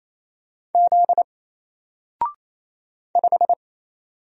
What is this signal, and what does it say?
Morse code, heard as a keyed tone, about 27 wpm, 700 Hz. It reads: Z 5
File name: Single-Letter-Number-28wpm